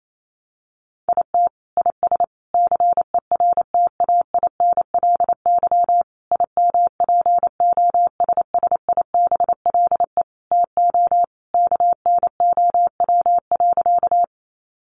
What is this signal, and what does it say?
Morse code, heard as a keyed tone, about 28 wpm, 700 Hz. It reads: IT IS CERTAINLY IMPOSSIBLE TO KNOW.
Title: Sentences from Top 1100 Words 28wpm